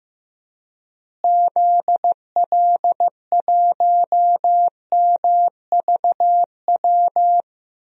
Morse code, heard as a keyed tone, about 15 wpm, 700 Hz. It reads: ZL1MVW
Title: DXCC-100 Call Signs - 6 Characters - Repeated 3x - 15wpm